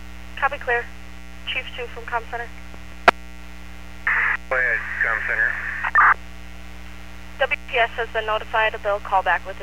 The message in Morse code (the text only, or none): none